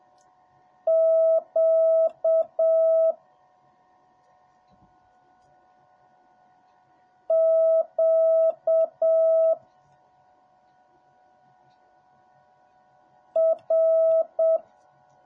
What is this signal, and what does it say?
Morse code, heard as a keyed tone, about 7 wpm, 650 Hz. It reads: Q Q R